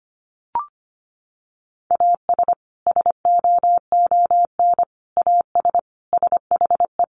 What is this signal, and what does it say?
Morse code, heard as a keyed tone, about 25 wpm, 700 Hz. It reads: AS SOON AS SHE
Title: Sentences from Top 1100 Words 25wpm